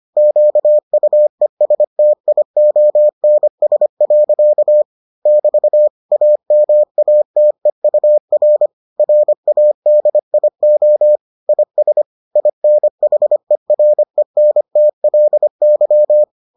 Morse code, heard as a keyed tone, about 25 wpm, 600 Hz. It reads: QUESTIONS. <BT> AMATEUR RADIO IS INHERENTLY